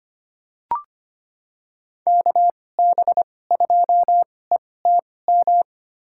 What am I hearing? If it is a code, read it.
Morse code, heard as a keyed tone, about 25 wpm, spaced as for 18 wpm, 700 Hz. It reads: KB2ETM
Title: USA Call Signs - 6 Characters - 2x Character Spacing - 25wpm